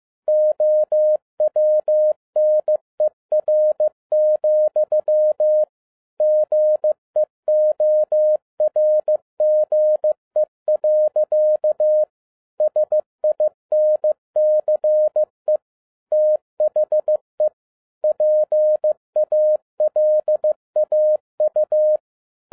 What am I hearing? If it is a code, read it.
Morse code, heard as a keyed tone, about 15 wpm, 610 Hz. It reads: OWNER, GEORGE. SINCE THE PALAU
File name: 2011-07-mp3 / n12 T8XO 15wpm